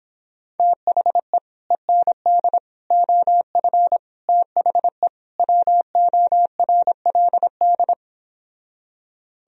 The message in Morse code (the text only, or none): THE END OF THE WORLD